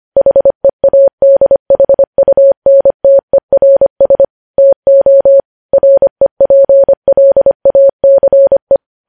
Morse code, heard as a keyed tone, about 25 wpm, 550 Hz. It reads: HEADHUNTERS TO REPLACE
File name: podcast / News headlines-1x-25x25